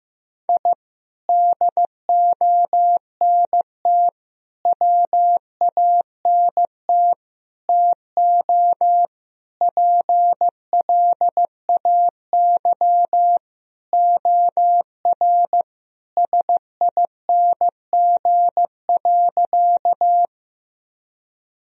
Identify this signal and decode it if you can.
Morse code, heard as a keyed tone, about 15 wpm, 700 Hz. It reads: I DONT WANT TO PLAY OR SING.
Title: Sentences from Top 1000 Words - Review Part 2 - 15wpm